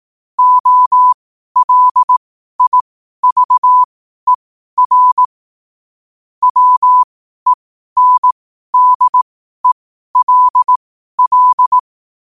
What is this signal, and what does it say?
Morse code, heard as a keyed tone, about 18 wpm, spaced as for 12 wpm, 1000 Hz. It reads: OLIVER WENDELL
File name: qotdfiles / 12wpm